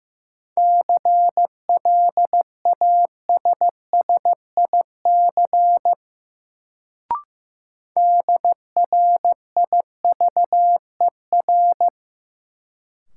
Morse code, heard as a keyed tone, about 15 wpm, 700 Hz. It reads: CLASSIC DRIVER
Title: Sentences from Top 1000 Words - New Word Drill - 15wpm